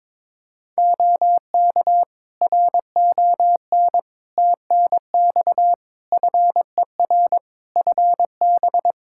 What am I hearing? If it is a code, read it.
Morse code, heard as a keyed tone, about 22 wpm, 700 Hz. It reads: OK RON TNX FER FB
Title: Practice QSO - Exchange 4 - Ending - 22wpm